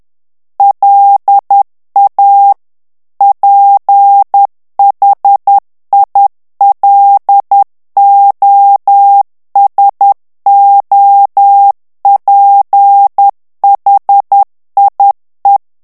Morse code, heard as a keyed tone, about 10 wpm, 800 Hz. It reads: LA PHILOSOPHIE
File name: morse mp3 / FRANCAIS TEXTE CLAIR 2 VITESSE 10 MOTS MINUTES